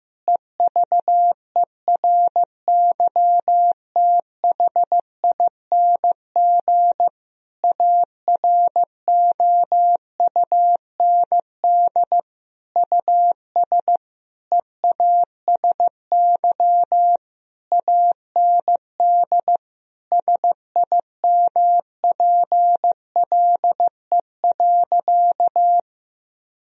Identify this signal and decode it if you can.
Morse code, heard as a keyed tone, about 15 wpm, 700 Hz. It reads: EVERYTHING AROUND US EASY AND SIMPLE.
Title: Sentences from Top 1000 Words - Review Part 3 - 15wpm